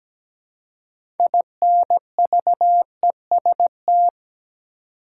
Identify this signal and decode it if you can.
Morse code, heard as a keyed tone, about 17 wpm, 700 Hz. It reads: INVEST
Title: Six Letter Words - 17wpm